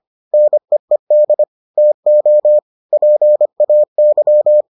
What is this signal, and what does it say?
Morse code, heard as a keyed tone, about 25 wpm, 600 Hz. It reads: NEED TO PAY